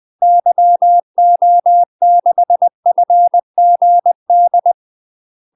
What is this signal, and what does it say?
Morse code, heard as a keyed tone, about 20 wpm, 700 Hz. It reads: YO6FGD
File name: UK MASTER QSOs TRAINING / 20WPM / UK MASTER QSOs 0047